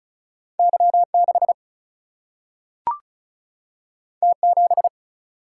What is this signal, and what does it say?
Morse code, heard as a keyed tone, about 36 wpm, 700 Hz. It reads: Y 6 T 7
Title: Single-Letter Single-Number 35wpm